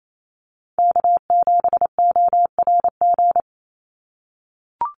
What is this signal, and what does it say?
Morse code, heard as a keyed tone, about 28 wpm, 700 Hz. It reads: K7ORG
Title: NA SOTA Chaser Callsigns 28wpm